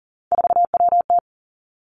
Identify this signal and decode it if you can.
Morse code, heard as a keyed tone, about 40 wpm, 700 Hz. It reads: 4WT
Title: Three Challenging Letter-Number Combinations 40wpm